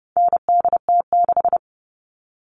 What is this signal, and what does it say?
Morse code, heard as a keyed tone, about 30 wpm, 700 Hz. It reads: NDT6
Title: All Cut Numbers - 4-Digit 30wpm